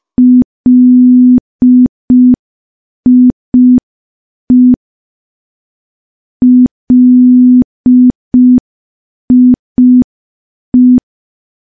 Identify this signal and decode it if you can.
Morse code, heard as a keyed tone, about 5 wpm, 250 Hz. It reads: LIE LIE
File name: morse / morse-training-05WPM